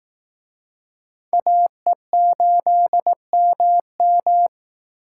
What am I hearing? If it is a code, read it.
Morse code, heard as a keyed tone, about 18 wpm, 700 Hz. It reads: AE8MM
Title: Speed Racing - USA 2x2 Call Signs 15wpm